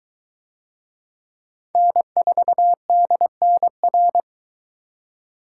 Morse code, heard as a keyed tone, about 22 wpm, 700 Hz. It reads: N4DNR